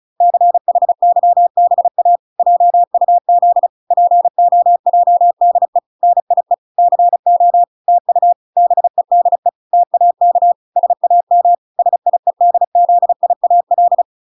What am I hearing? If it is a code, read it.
Morse code, heard as a keyed tone, about 34 wpm, 700 Hz. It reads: CHYBA JUZ POJDE NIE CO TU BEDE TAK SAM SIEDZIAL